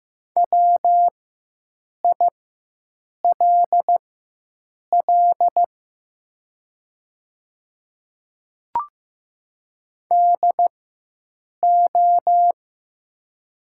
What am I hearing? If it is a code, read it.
Morse code, heard as a keyed tone, about 15 wpm, spaced as for 7 wpm, 700 Hz. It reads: WILL DO
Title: 4x Character Spacing - Top 100 Words - 15wpm